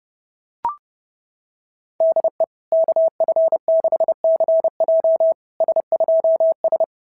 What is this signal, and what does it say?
Morse code, heard as a keyed tone, about 31 wpm, 660 Hz. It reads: DE KF6CJ S2S
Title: SOTA QSO Practice 30wpm